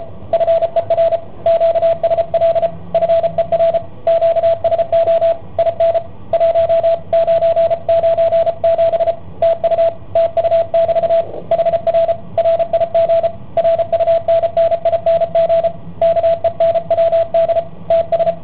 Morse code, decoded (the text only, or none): FER QSL FER QSO IN 1997 TU TU<BT> HR RIG RUNNING KENWD TS